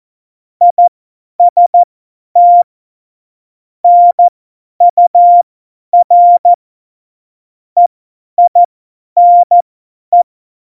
Morse code, heard as a keyed tone, about 14 wpm, spaced as for 10 wpm, 700 Hz. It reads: IST NUR EINE